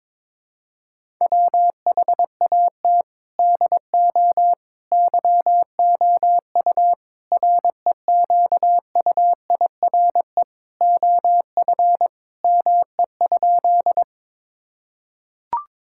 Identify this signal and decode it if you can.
Morse code, heard as a keyed tone, about 21 wpm, 700 Hz. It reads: WHAT DO YOU REQUIRE OF ME?
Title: Sentences from Top 1000 Words - Review Part 6 - 22wpm